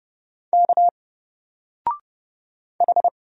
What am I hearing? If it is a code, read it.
Morse code, heard as a keyed tone, about 30 wpm, 700 Hz. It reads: K H
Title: Single Letter-Number 30wpm